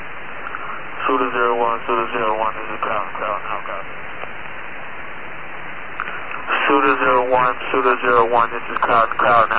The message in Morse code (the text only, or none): none